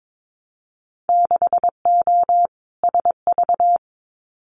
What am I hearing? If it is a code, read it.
Morse code, heard as a keyed tone, about 22 wpm, 700 Hz. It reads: 6O SV
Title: Winter Field Day Exchanges Encore 22wpm